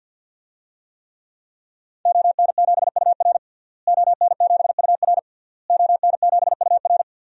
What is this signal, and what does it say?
Morse code, heard as a keyed tone, about 52 wpm, 700 Hz. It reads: ON7UR ON7UR ON7UR
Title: DXCC-100 Call Signs - 5 Characters - Repeated 3x - 50wpm